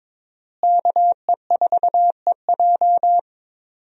Morse code, heard as a keyed tone, about 22 wpm, 700 Hz. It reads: KE4EJ
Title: USA Call Signs - 5 Characters - Repeated 3x - 22wpm